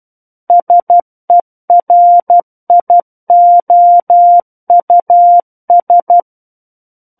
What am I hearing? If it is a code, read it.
Morse code, heard as a keyed tone, about 12 wpm, 700 Hz. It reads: SERIOUS
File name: Book One / 03-Chapter 3 - 12 WPM